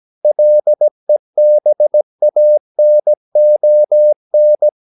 Morse code, heard as a keyed tone, about 17 wpm, 600 Hz. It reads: LEBANON
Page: Romanen Never af Ken Follet på engelsk.